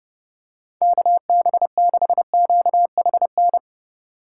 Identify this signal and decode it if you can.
Morse code, heard as a keyed tone, about 29 wpm, 700 Hz. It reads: KB6QHN